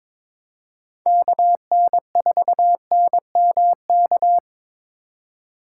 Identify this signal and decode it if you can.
Morse code, heard as a keyed tone, about 27 wpm, 700 Hz. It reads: KN4NMK
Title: Speed Racing - USA 2x3 Call Signs 22wpm